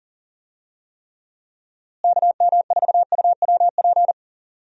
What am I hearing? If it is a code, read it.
Morse code, heard as a keyed tone, about 40 wpm, 700 Hz. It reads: KM4UWP
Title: USA Call Signs - 6 Characters - Repeated 3x - 40wpm